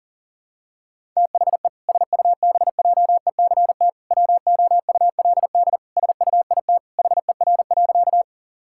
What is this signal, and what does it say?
Morse code, heard as a keyed tone, about 40 wpm, 700 Hz. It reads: THE SUBJECT WOULD SUIT HER.
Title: Sentences from Top 1600 Words 40wpm